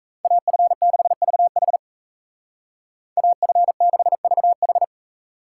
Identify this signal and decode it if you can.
Morse code, heard as a keyed tone, about 34 wpm, 700 Hz. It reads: AF6VH AF6VH
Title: Speed Racing - USA 2x2 Call Signs 28wpm